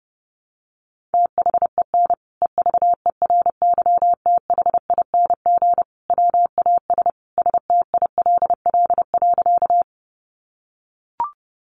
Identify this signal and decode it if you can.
Morse code, heard as a keyed tone, about 30 wpm, 700 Hz. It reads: THEN EVERYTHING WAS STILL.
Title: Sentences from Top 700 Words 30wpm